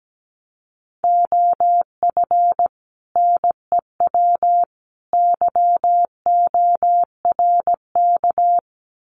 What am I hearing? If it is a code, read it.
Morse code, heard as a keyed tone, about 17 wpm, 700 Hz. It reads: OF NEW YORK
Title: Sets of 3 Words 17wpm